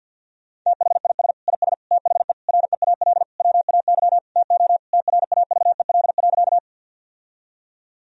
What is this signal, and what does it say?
Morse code, heard as a keyed tone, about 50 wpm, 700 Hz. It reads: THIS IS THE REAL WAY TO TRAVEL.
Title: Sentences from Top 1000 Words - Review Part 2 - 50wpm